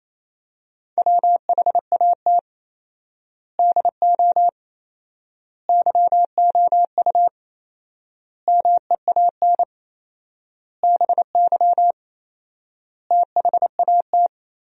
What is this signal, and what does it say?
Morse code, heard as a keyed tone, about 28 wpm, 700 Hz. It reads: WHAT DO YOU MEAN BY THAT
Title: 4x Word-Spacing - Sentences from Top 400 Words - 28wpm